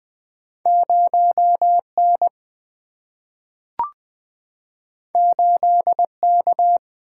Morse code, heard as a keyed tone, about 20 wpm, 700 Hz. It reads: 0N 8K
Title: Single-Letter Single-Number 20wpm